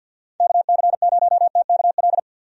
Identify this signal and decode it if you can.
Morse code, heard as a keyed tone, about 47 wpm, 700 Hz. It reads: KC0TKL
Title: USA Call Signs - 6 Characters - 50wpm